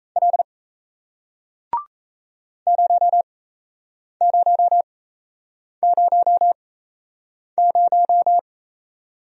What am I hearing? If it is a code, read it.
Morse code, heard as a keyed tone, about 37 wpm, spaced as for 8 wpm, 700 Hz. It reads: L 0000